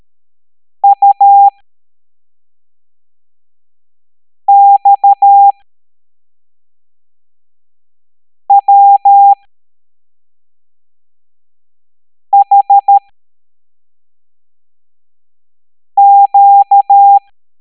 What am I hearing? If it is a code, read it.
Morse code, heard as a keyed tone, about 13 wpm, spaced as for 3 wpm, 800 Hz. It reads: UXWHQ
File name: hamdata / L07-EX02-HQP